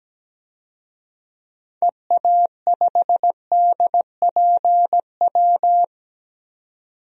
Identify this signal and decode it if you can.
Morse code, heard as a keyed tone, about 17 wpm, 700 Hz. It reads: EA5DPW
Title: DXCC-100 Call Signs - 6 Characters - 17wpm